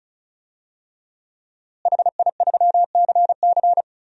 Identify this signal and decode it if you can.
Morse code, heard as a keyed tone, about 34 wpm, 700 Hz. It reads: HI3CC
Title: DXCC-100 Call Signs - 5 Characters - Repeated 3x - 35wpm